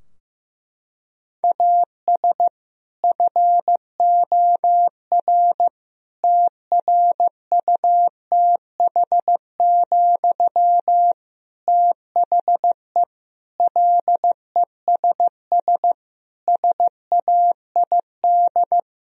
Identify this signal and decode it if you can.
Morse code, heard as a keyed tone, about 15 wpm, 700 Hz. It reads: AS FOR TRUTH, THE LESS SAID
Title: Sentences from Top 1000 Words - Review Part 2 - 15wpm